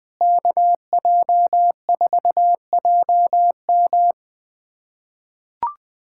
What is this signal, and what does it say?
Morse code, heard as a keyed tone, about 20 wpm, 700 Hz. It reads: KJ4JM
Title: USA Callsign DE Callsign - 5 Characters - 20wpm